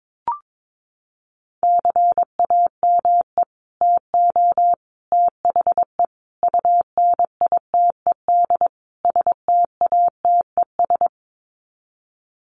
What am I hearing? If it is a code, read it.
Morse code, heard as a keyed tone, about 22 wpm, 700 Hz. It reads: CAME TO THE UNITED STATES
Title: Sets of 5 Words Encore2 22wpm